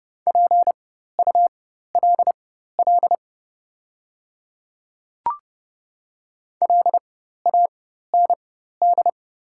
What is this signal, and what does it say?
Morse code, heard as a keyed tone, about 29 wpm, spaced as for 14 wpm, 690 Hz. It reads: PULL LAND
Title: 4x Character Spacing - Top 500 Words - 30wpm